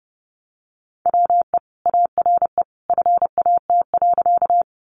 Morse code, heard as a keyed tone, about 30 wpm, 700 Hz. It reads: WE ARE FAT.
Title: Sentences from Top 600 Words 30wpm